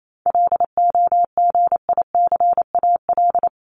Morse code, heard as a keyed tone, about 28 wpm, 700 Hz. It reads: LOGICAL
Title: Top 1501-1750 Words 28wpm